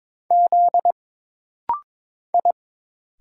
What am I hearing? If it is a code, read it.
Morse code, heard as a keyed tone, about 22 wpm, 700 Hz. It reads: Z I
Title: Single Letter-Number Alt - Rapid-Fire 22wpm